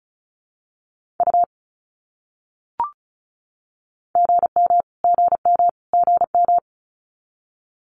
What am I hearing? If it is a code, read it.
Morse code, heard as a keyed tone, about 34 wpm, 700 Hz. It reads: U GM GM GM
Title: Abbreviations Repeated 3x - 35wpm